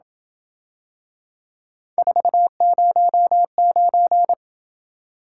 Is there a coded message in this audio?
Yes: Morse code from 2.0 to 4.3 s, heard as a keyed tone, about 26 wpm, 700 Hz. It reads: 409